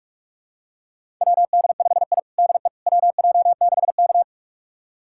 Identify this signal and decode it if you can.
Morse code, heard as a keyed tone, about 42 wpm, 700 Hz. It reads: WD5I DE WJ6K
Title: USA Callsign DE Callsign - 4 Characters - 45wpm